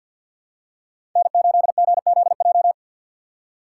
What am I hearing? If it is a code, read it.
Morse code, heard as a keyed tone, about 51 wpm, 700 Hz. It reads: N8GZJ
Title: USA Call Signs - 2x Character Spacing - 50wpm